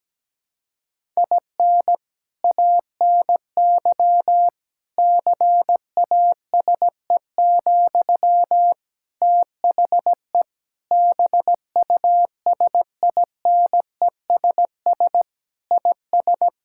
Morse code, heard as a keyed tone, about 17 wpm, 700 Hz. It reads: IN ANY CASE, THE BUSINESS IS
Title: Sentences from Top 1000 Words - Review Part 5 - 17wpm